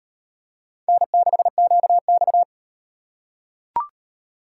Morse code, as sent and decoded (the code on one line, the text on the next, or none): -. -.... --.- -..-
N6QX